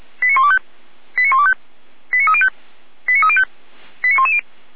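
Five-tone selective call (ZVEI), heard as quick runs of five tones, tones of about 70 ms. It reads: 89125 89125 89305 89305 89110
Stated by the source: ZVEI five-tone selective call received via SDR.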